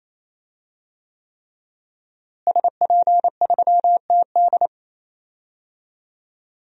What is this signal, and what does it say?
Morse code, heard as a keyed tone, about 27 wpm, 700 Hz. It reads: SP3TD
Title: DXCC-100 Call Signs - 5 Characters - Repeated 3x - 28wpm